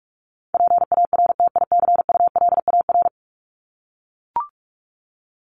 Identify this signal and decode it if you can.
Morse code, heard as a keyed tone, about 44 wpm, 700 Hz. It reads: PARTICULAR
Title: LY Suffix - Top 100 Words with Base Words 45wpm